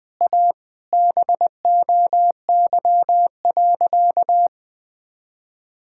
Morse code, heard as a keyed tone, about 21 wpm, 700 Hz. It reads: A BOY.
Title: Sentences from Top 1600 Words 20wpm